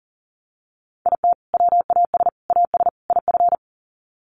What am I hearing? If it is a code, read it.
Morse code, heard as a keyed tone, about 40 wpm, 700 Hz. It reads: IT WAS AS IF
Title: Sets of 4 Words 40wpm